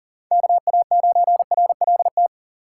Morse code, heard as a keyed tone, about 40 wpm, 700 Hz. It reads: KA9RLT
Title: USA Call Signs - 6x Character Spacing - 40wpm